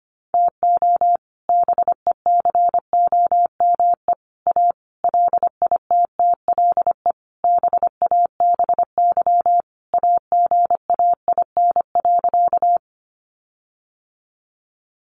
Morse code, heard as a keyed tone, about 25 wpm, 700 Hz. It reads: TO BECOME A LITTLE BABY AGAIN.
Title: Sentences from Top 1300 Words 25wpm